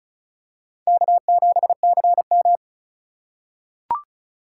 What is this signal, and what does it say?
Morse code, heard as a keyed tone, about 34 wpm, 700 Hz. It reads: K7CM
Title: USA Call Signs - 4 Characters - 3x Character Spacing - 35wpm